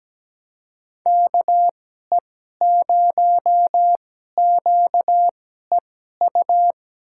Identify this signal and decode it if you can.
Morse code, heard as a keyed tone, about 17 wpm, spaced as for 12 wpm, 700 Hz. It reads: KE0QEU
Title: USA Call Signs - 6 Characters - 2x Character Spacing - 17wpm